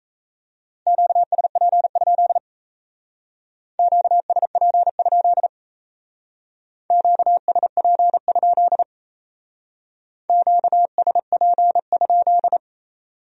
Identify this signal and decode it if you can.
Morse code, heard as a keyed tone, about 35 wpm, 700 Hz. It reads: QSP? QSP? QSP? QSP?
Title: Speed-Racing Q-Codes - 28wpm